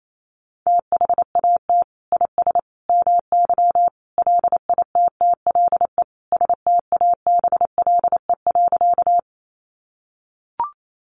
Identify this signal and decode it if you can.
Morse code, heard as a keyed tone, about 28 wpm, 700 Hz. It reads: THAT IS MY LITTLE STABLE.
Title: Sentences from Top 1500 Words 28wpm